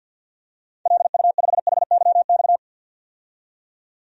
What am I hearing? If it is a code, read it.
Morse code, heard as a keyed tone, about 51 wpm, 700 Hz. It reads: LU5HYX
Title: DXCC-100 Call Signs - 6 Characters - 50wpm